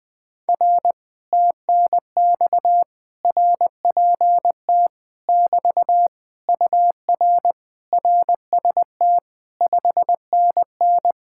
Morse code, heard as a keyed tone, about 20 wpm, 700 Hz. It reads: R TNX RPT <BT> UR RST 5NN
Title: Speed-Racing Common QSO Phrases - 17wpm